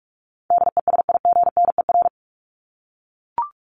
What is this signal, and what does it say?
Morse code, heard as a keyed tone, about 44 wpm, 700 Hz. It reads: DESIGNER
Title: Top 1501-1750 Words 45wpm